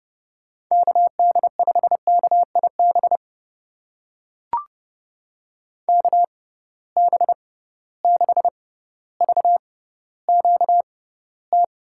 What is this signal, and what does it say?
Morse code, heard as a keyed tone, about 32 wpm, spaced as for 16 wpm, 700 Hz. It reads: KD5KIB KB6VQT
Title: USA Call Signs - 6 Characters - 6x Character Spacing - 30wpm